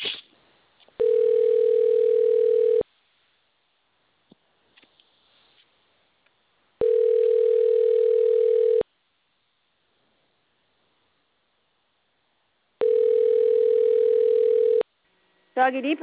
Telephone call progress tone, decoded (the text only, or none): ringback tone